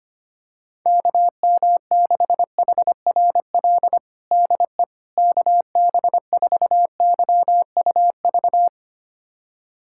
Morse code, heard as a keyed tone, about 25 wpm, 700 Hz. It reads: KM6HRL DE KB4YUV
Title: USA Callsign DE Callsign 25wpm